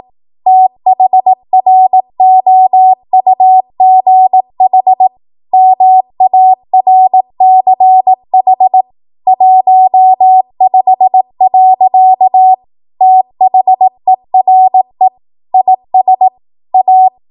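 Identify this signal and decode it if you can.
Morse code, heard as a keyed tone, about 18 wpm, 750 Hz. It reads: THROUGH MARCH 15. THERE IS A
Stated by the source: Code practice at 18 WPM, a transition run bridging 15 to 20 WPM.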